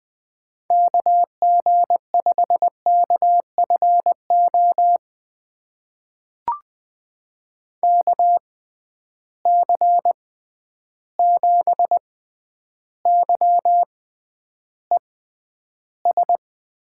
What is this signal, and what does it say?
Morse code, heard as a keyed tone, about 20 wpm, spaced as for 10 wpm, 700 Hz. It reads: KG5KFO KC7YES
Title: USA Call Signs - 6 Characters - 6x Character Spacing - 20wpm